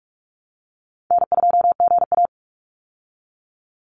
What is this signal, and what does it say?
Morse code, heard as a keyed tone, about 48 wpm, 700 Hz. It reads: N2GA
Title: Easy Call Signs Repeated 2x - 45wpm